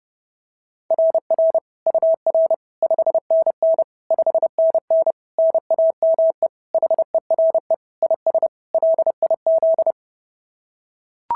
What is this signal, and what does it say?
Morse code, heard as a keyed tone, about 30 wpm, 690 Hz. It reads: RR UR 5NN 5NN NAME HERE IS LIZ
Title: SOTA QSO Practice 30wpm